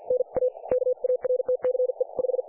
Morse code, heard as a keyed tone, about 50 wpm, spaced as for 43 wpm, 500 Hz. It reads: NA KARTCE 5
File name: dlazaawansowanych / nagraniazpasma / websdr recording 3564.7kHz